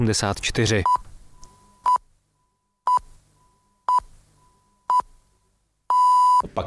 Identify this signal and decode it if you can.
Radio time signal (time pips), with speech. It is heard from 0.9 to 6.4 s.